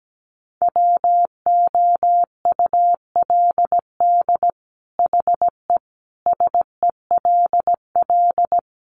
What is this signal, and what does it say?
Morse code, heard as a keyed tone, about 17 wpm, 700 Hz. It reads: WOULD HE SELL
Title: Sentences from Top 1100 Words 17wpm